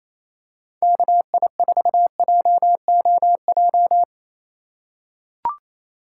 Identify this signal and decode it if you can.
Morse code, heard as a keyed tone, about 28 wpm, 700 Hz. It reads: KI4JOJ
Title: USA Call Signs - 6 Characters - 2x Character Spacing - 28wpm